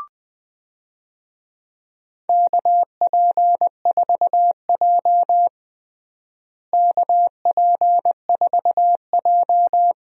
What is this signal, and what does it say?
Morse code, heard as a keyed tone, about 20 wpm, 700 Hz. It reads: KP4J KP4J
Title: DXCC-100 Call Signs - 4 Characters - Repeated 2x - 20wpm